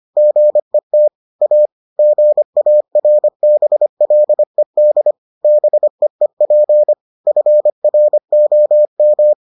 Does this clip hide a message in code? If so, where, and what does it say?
Yes: Morse code from 0.2 to 9.3 s, heard as a keyed tone, about 25 wpm, 600 Hz. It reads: GET A GARBLED BEEP FROM